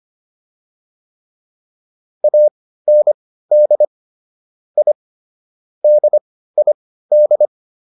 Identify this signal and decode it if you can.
Morse code, heard as a keyed tone, about 25 wpm, spaced as for 15 wpm, 600 Hz. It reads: AND I DID